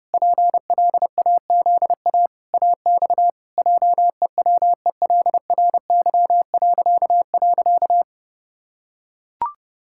Morse code, heard as a keyed tone, about 30 wpm, 700 Hz. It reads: PLAZA AX JEWELRY..
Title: Pangram Challenge v2 - 30wpm